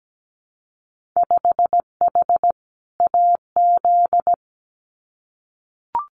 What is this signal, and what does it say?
Morse code, heard as a keyed tone, about 17 wpm, 700 Hz. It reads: 5H AZ
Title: Winter Field Day Exchanges Encore 17wpm